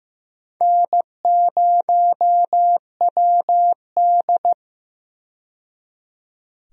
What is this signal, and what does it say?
Morse code, heard as a keyed tone, about 15 wpm, 700 Hz. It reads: N0WD
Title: USA Call Signs - 4 Characters - 15wpm